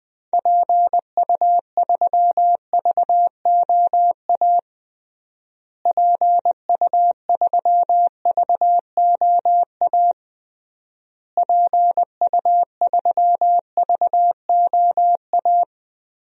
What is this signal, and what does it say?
Morse code, heard as a keyed tone, about 20 wpm, 700 Hz. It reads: PU3VOA PU3VOA PU3VOA
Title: DXCC-100 Call Signs - 6 Characters - Repeated 3x - 20wpm